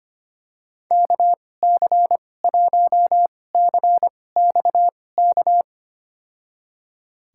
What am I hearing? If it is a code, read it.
Morse code, heard as a keyed tone, about 25 wpm, spaced as for 18 wpm, 700 Hz. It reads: KC1CXK
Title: USA Call Signs - 6 Characters - 2x Character Spacing - 25wpm